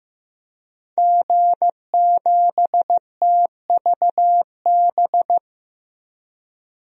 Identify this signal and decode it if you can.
Morse code, heard as a keyed tone, about 15 wpm, 700 Hz. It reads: G7TVB
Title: DXCC-100 Call Signs - 5 Characters - Repeated 3x - 15wpm